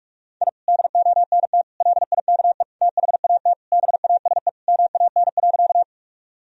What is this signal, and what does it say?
Morse code, heard as a keyed tone, about 46 wpm, 700 Hz. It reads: I DONT LIKE THAT BASE MAN.
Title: Sentences from Top 1000 Words - Review Part 1 - 45wpm